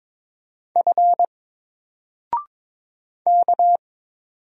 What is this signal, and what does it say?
Morse code, heard as a keyed tone, about 22 wpm, 700 Hz. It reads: F K
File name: Single-Letter-Number-22wpm